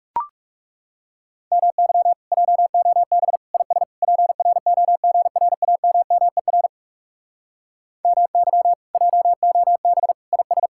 Morse code, heard as a keyed tone, about 40 wpm, 700 Hz. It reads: MY JOB IS PROGRAMMER MY JOB IS